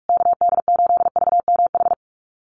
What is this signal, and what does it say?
Morse code, heard as a keyed tone, about 44 wpm, 700 Hz. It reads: KD8VMH
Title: USA Call Signs Repeated 3x 45wpm